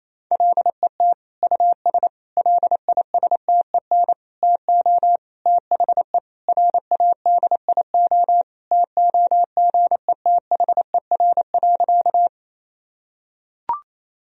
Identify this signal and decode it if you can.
Morse code, heard as a keyed tone, about 27 wpm, 700 Hz. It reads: LET US LISTEN TO THE RADIO TOGETHER.
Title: Sentences from Top 1000 Words - Review Part 3 - 28wpm